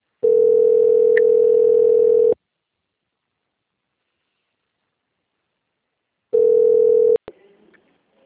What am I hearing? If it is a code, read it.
Ringback tone (North America), 440 + 480 Hz.